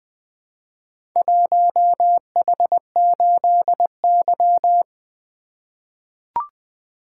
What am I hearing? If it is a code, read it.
Morse code, heard as a keyed tone, about 20 wpm, 700 Hz. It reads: 1H8Y
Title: Four Letter-Number Combinations - 20wpm